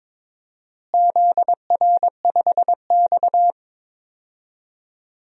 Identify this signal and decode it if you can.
Morse code, heard as a keyed tone, about 21 wpm, 700 Hz. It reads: ZR5X
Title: DXCC-100 Call Signs - 4 Characters - Repeated 3x - 22wpm